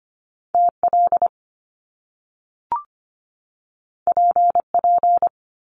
Morse code, heard as a keyed tone, about 25 wpm, 700 Hz. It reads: TL PP
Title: Top Two Letter Combinations in Words 25wpm